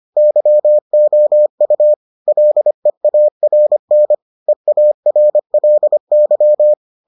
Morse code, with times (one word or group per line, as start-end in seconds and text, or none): 0.2-1.9 YOU
2.3-4.1 LEARN
4.5-6.7 EARLY